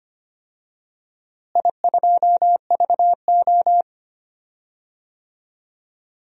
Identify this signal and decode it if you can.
Morse code, heard as a keyed tone, about 25 wpm, 700 Hz. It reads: I2VO
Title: DXCC-100 Call Signs - 4 Characters - Repeated 2x - 25wpm